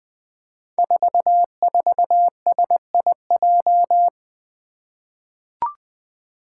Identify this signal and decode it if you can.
Morse code, heard as a keyed tone, about 20 wpm, 700 Hz. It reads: 44SIJ
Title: Five Challenging Letter-Number Combinations 20wpm